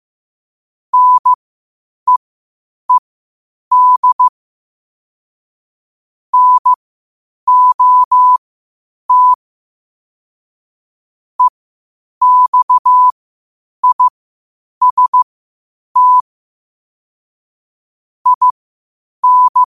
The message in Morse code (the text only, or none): NEED NOT EXIST IN